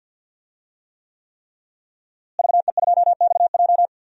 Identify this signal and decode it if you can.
Morse code, heard as a keyed tone, about 48 wpm, 700 Hz. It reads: VE2XJ
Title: DXCC-100 Call Signs - 5 Characters - Repeated 3x - 50wpm